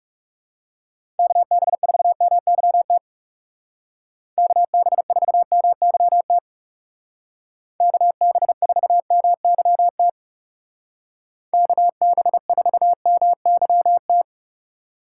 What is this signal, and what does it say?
Morse code, heard as a keyed tone, about 35 wpm, 700 Hz. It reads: KB4MYT KB4MYT KB4MYT KB4MYT
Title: Speed Racing - USA 2x3 Call Signs 30wpm